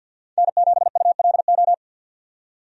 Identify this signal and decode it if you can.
Morse code, heard as a keyed tone, about 48 wpm, 700 Hz. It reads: N7ULO
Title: USA Call Signs - 5 Characters - 50wpm